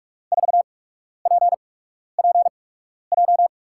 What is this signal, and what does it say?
Morse code, heard as a keyed tone, about 44 wpm, spaced as for 12 wpm, 700 Hz. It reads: 4PPJ
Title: USA Call Signs - 6 Characters - 8x Character Spacing - 45wpm